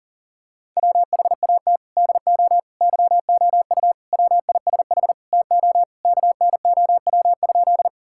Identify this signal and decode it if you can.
Morse code, heard as a keyed tone, about 40 wpm, 700 Hz. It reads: WHAT DO YOU WISH TO KNOW?
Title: Sentences from Top 1600 Words 40wpm